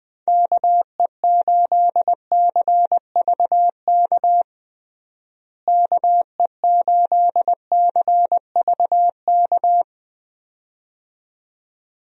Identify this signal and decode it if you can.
Morse code, heard as a keyed tone, about 20 wpm, 700 Hz. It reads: KE8CVK KE8CVK
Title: USA Call Signs - 6 Characters - Repeated 2x - 20wpm